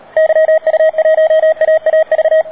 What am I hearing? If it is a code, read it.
Morse code, heard as a keyed tone, about 36 wpm, 660 Hz. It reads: YU1AAV
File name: D4C@10CW